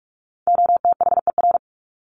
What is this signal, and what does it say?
Morse code, heard as a keyed tone, about 44 wpm, 700 Hz. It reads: OTHER
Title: Top 300 Words 45wpm